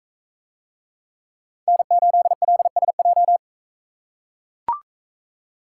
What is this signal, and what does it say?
Morse code, heard as a keyed tone, about 39 wpm, 700 Hz. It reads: N8LSJ